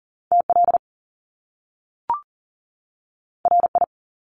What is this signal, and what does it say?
Morse code, heard as a keyed tone, about 42 wpm, 700 Hz. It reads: TL RI
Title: Top Two Letter Combinations in Words 40wpm